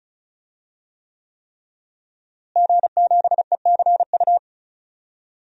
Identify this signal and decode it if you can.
Morse code, heard as a keyed tone, about 36 wpm, 700 Hz. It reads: G7ECU
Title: DXCC-100 Call Signs - 5 Characters - Repeated 3x - 35wpm